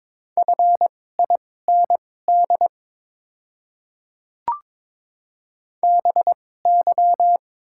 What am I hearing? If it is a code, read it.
Morse code, heard as a keyed tone, about 22 wpm, spaced as for 16 wpm, 700 Hz. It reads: FIND BY
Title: 2x Character Spacing - Top 200 Words - 22wpm